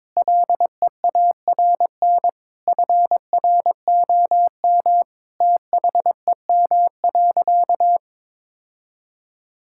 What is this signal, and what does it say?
Morse code, heard as a keyed tone, about 22 wpm, 700 Hz. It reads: LEARN FROM THEM.
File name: Difficult Sentences with Hints - 22wpm